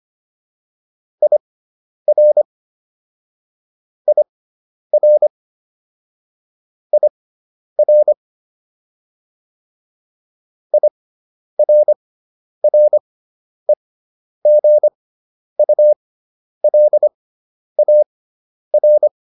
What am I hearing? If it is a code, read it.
Morse code, heard as a keyed tone, about 25 wpm, spaced as for 10 wpm, 600 Hz. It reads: IR IR IR IRREGULAR